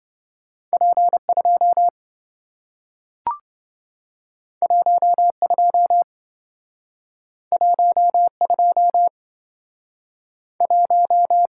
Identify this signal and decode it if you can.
Morse code, heard as a keyed tone, about 27 wpm, spaced as for 4 wpm, 700 Hz. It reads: P 21 21 21